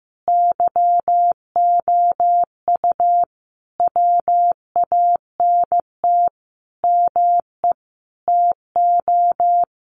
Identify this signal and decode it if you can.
Morse code, heard as a keyed tone, about 15 wpm, 700 Hz. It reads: YOU WANT ME TO
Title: Sentences from Top 200 Words 15wpm